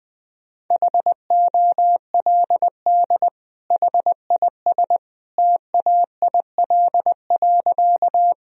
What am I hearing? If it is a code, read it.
Morse code, heard as a keyed tone, about 21 wpm, 700 Hz. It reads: HOLD HIS TAIL.
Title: Sentences from Top 1000 Words - Review Part 3 - 20wpm